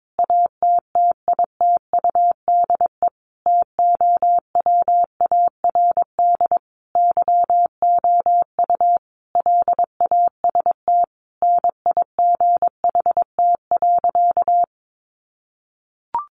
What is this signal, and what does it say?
Morse code, heard as a keyed tone, about 22 wpm, 700 Hz. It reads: ATTITUDE TOWARD YOU LAST NIGHT.
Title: Sentences from Top 1300 Words 22wpm